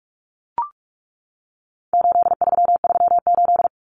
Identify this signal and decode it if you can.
Morse code, heard as a keyed tone, about 44 wpm, 700 Hz. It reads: 8338
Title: Numbers - 4-Digit 45wpm